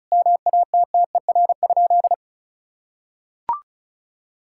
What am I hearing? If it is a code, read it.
Morse code, heard as a keyed tone, about 36 wpm, 700 Hz. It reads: MATTER?
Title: Sentences from Top 1000 Words - Review Part 5 - 35wpm